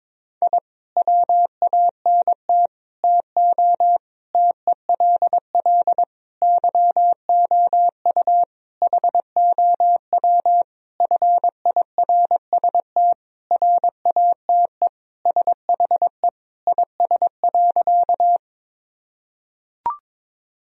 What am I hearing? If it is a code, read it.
Morse code, heard as a keyed tone, about 22 wpm, 700 Hz. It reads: I WANT TO TELL YOU HOW FIRST RATE SHE IS.
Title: Sentences from Top 1000 Words - Review Part 4 - 22wpm